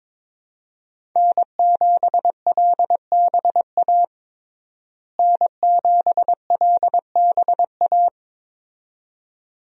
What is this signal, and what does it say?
Morse code, heard as a keyed tone, about 22 wpm, 700 Hz. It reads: N7LBA N7LBA
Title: USA Call Signs - 5 Characters - Repeated 2x - 22wpm